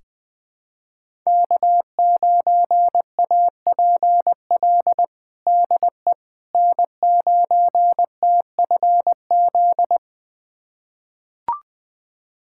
Morse code, heard as a keyed tone, about 20 wpm, 700 Hz. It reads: K9APL DE N9TFZ
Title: USA Callsign DE Callsign - 5 Characters - 20wpm